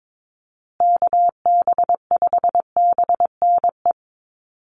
Morse code, heard as a keyed tone, about 22 wpm, 700 Hz. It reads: KB5BNE
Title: USA Call Signs Repeated 3x 22wpm